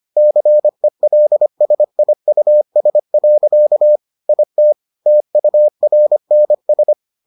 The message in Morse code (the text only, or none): CELSIUS. IT TURNS